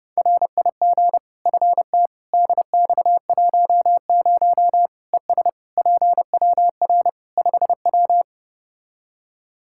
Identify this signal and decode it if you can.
Morse code, heard as a keyed tone, about 30 wpm, 700 Hz. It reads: RIG FT DX10 ES PWR 5W
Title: Practice QSO - Exchange 2 - Info - 30wpm